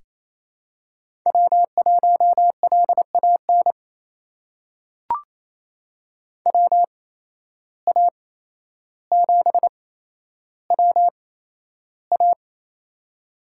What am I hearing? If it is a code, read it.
Morse code, heard as a keyed tone, about 28 wpm, spaced as for 12 wpm, 700 Hz. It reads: W1LAN WA7WA
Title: USA Call Signs - 5 Characters - 8x Character Spacing - 28wpm